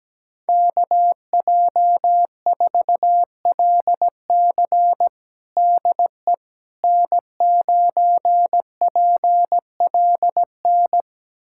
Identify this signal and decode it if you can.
Morse code, heard as a keyed tone, about 17 wpm, 700 Hz. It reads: KJ4LC DE N9PLN
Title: USA Callsign DE Callsign - 5 Characters - 17wpm